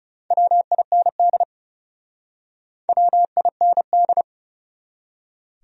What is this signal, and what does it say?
Morse code, heard as a keyed tone, about 40 wpm, 700 Hz. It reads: WIND WIND